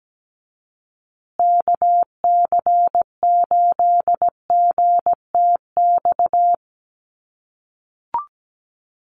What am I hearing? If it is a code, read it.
Morse code, heard as a keyed tone, about 17 wpm, 700 Hz. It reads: KC8GTX
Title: USA Call Signs Repeated 2x 17wpm